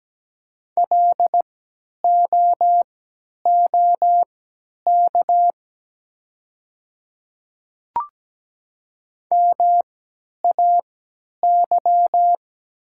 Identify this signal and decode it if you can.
Morse code, heard as a keyed tone, about 17 wpm, spaced as for 10 wpm, 700 Hz. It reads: LOOK MAY
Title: 3x Character Spacing - Top 100 Words - 17wpm